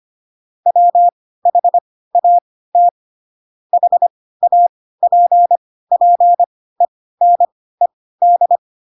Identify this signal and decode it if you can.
Morse code, heard as a keyed tone, about 25 wpm, spaced as for 16 wpm, 700 Hz. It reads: WHAT HAPPENED
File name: Pinocchio / Pinocchio ch3 10 25 ik1hge